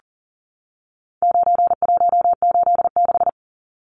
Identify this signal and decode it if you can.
Morse code, heard as a keyed tone, about 40 wpm, 700 Hz. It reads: 9186
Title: Numbers - 4-Digit 40wpm